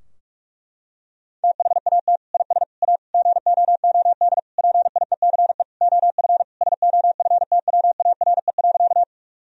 Morse code, heard as a keyed tone, about 47 wpm, 700 Hz. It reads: THAT IS A GOOD PIECE OF SOFTWARE.
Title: Sentences from Top 1000 Words - Review Part 3 - 45wpm